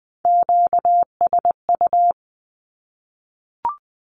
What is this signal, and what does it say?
Morse code, heard as a keyed tone, about 20 wpm, 700 Hz. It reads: QSU
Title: Q-Codes Repeated 3x 20wpm